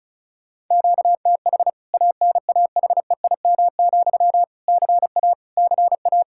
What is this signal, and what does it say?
Morse code, heard as a keyed tone, about 36 wpm, 700 Hz. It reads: QTH ANAHEIM, CA CA
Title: K1USN SST QSO Practice 35wpm